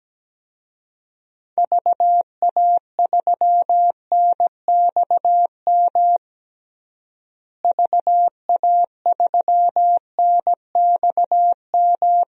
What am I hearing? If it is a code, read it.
Morse code, heard as a keyed tone, about 17 wpm, 700 Hz. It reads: VA3NXM VA3NXM
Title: DXCC-100 Call Signs - 6 Characters - Repeated 2x - 17wpm